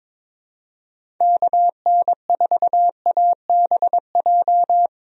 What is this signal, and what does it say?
Morse code, heard as a keyed tone, about 22 wpm, 700 Hz. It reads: KN4ABJ
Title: USA Call Signs - 6 Characters - 22wpm